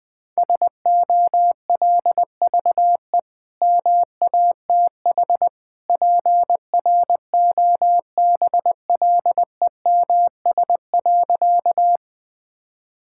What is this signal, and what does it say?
Morse code, heard as a keyed tone, about 20 wpm, 700 Hz. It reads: SOLVE MATH PROBLEMS.
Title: Easy Sentences with Hints - 20wpm